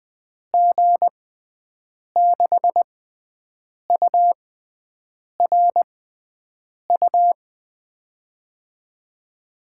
Morse code, heard as a keyed tone, about 20 wpm, spaced as for 7 wpm, 700 Hz. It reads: G6URU
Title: USA Call Signs - 6x Character Spacing - 20wpm